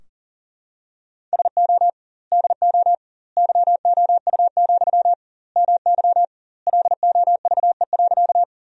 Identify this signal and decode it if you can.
Morse code, heard as a keyed tone, about 37 wpm, 700 Hz. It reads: SO DO YOU, MY LOVE.
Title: 2x Word-Spacing - Sentences from Top 400 Words - 40wpm